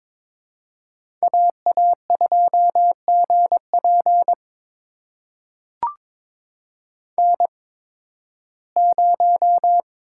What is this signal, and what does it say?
Morse code, heard as a keyed tone, about 22 wpm, spaced as for 14 wpm, 700 Hz. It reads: AA2GP N0
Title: USA Call Signs - 8x Character Spacing - 22wpm